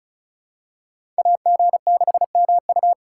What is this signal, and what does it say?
Morse code, heard as a keyed tone, about 36 wpm, 700 Hz. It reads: AG6MU
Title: USA Call Signs - 2x Character Spacing - 35wpm